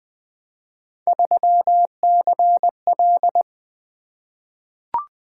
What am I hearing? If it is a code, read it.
Morse code, heard as a keyed tone, about 20 wpm, 700 Hz. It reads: 3CL
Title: Three Letter-Number Combinations - 20wpm